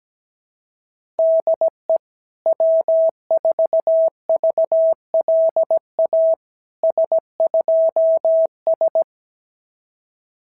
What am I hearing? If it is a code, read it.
Morse code, heard as a keyed tone, about 17 wpm, 650 Hz. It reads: DE W4VLA S2S
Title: SOTA QSO Practice 17wpm